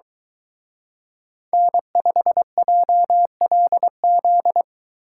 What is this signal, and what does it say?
Morse code, heard as a keyed tone, about 25 wpm, 700 Hz. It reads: N5JLZ